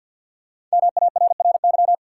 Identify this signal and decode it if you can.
Morse code, heard as a keyed tone, about 47 wpm, 700 Hz. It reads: MARRY
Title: Sentences from Top 1200 Words - New Word Drill - 50wpm